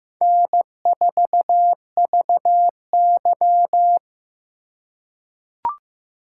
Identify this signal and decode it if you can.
Morse code, heard as a keyed tone, about 15 wpm, 700 Hz. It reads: N4VY
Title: USA Call Signs - 4 Characters - 3x Character Spacing - 15wpm